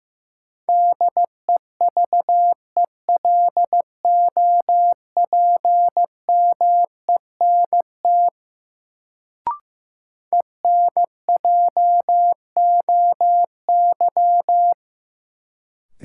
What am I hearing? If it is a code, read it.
Morse code, heard as a keyed tone, about 15 wpm, 700 Hz. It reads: DEVELOPMENT ENJOY
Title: Sentences from Top 1000 Words - Review Part 4 - New Word Drill - 15wpm